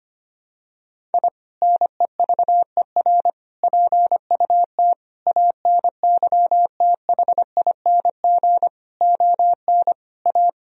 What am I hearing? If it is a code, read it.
Morse code, heard as a keyed tone, about 25 wpm, 700 Hz. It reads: I NEVER PUT ANYTHING ON A
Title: Sentences from Top 1000 Words - Review Part 1 - 25wpm